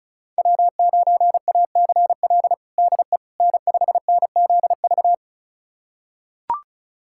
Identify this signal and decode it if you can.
Morse code, heard as a keyed tone, about 36 wpm, 700 Hz. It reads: W9ACL DE N5NZV
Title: USA Callsign DE Callsign - 5 Characters - 35wpm